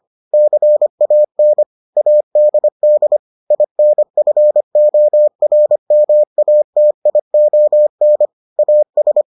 Morse code, heard as a keyed tone, about 25 wpm, 600 Hz. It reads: CAN ADD INFORMATION AS